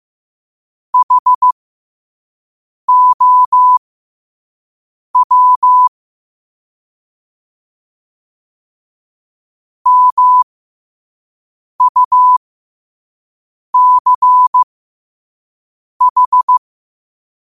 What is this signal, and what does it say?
Morse code, heard as a keyed tone, about 15 wpm, spaced as for 5 wpm, 1000 Hz. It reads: HOW MUCH
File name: qotdfiles / 5wpm